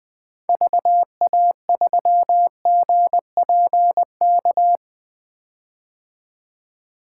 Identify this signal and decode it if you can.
Morse code, heard as a keyed tone, about 20 wpm, 700 Hz. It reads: VA3GPK
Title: 2023 POTA Activator Call Signs - 20wpm